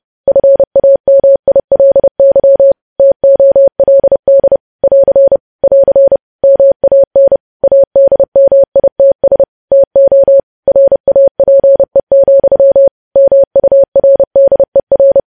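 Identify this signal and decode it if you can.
Morse code, heard as a keyed tone, about 30 wpm, 550 Hz. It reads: FAMILY TOLD <AR> <AR> MAN ADMITS TO RAPE, MURDER
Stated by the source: News headlines sent in Morse at 30.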